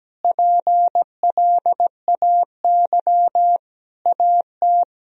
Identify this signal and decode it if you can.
Morse code, heard as a keyed tone, about 17 wpm, 700 Hz. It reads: PLAY AT
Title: Sentences from Top 1000 Words - Review Part 4 - 17wpm